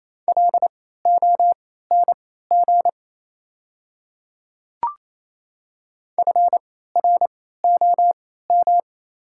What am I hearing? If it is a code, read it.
Morse code, heard as a keyed tone, about 28 wpm, spaced as for 16 wpm, 690 Hz. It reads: LONG FROM
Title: 3x Character Spacing - Top 100 Words - 28wpm